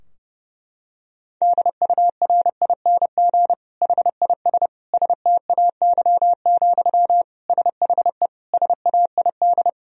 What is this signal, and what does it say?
Morse code, heard as a keyed tone, about 30 wpm, 700 Hz. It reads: DURING HIS STAY, SHE SAID
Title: Sentences from Top 1000 Words - Review Part 2 - 30wpm